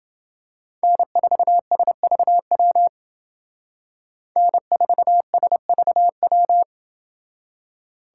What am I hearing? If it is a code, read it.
Morse code, heard as a keyed tone, about 28 wpm, 700 Hz. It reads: N4SVW N4SVW